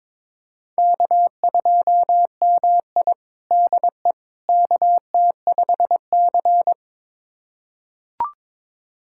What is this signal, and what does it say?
Morse code, heard as a keyed tone, about 22 wpm, 700 Hz. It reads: K2MI DE KT5C
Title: USA Callsign DE Callsign - 4 Characters - 22wpm